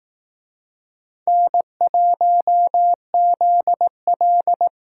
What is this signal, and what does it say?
Morse code, heard as a keyed tone, about 18 wpm, 700 Hz. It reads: N1ZL
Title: Speed Racing - USA 1x2 Call Signs 15wpm